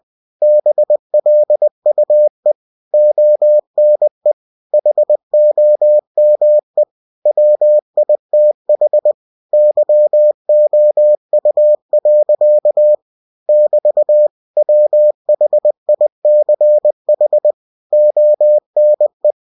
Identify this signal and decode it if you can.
Morse code, heard as a keyed tone, about 20 wpm, 600 Hz. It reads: BLUE ONE HOME WITH YOU. <BT> WHICH ONE
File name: cw-books / winnie-the-pooh / winnie-20wpm-page-0005